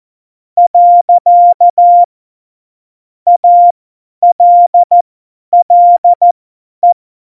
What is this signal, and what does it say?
Morse code, heard as a keyed tone, about 14 wpm, spaced as for 10 wpm, 700 Hz. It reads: . ALLE
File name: Morsekurs / Witze / Witz 23. Oktober 2018, 50 F70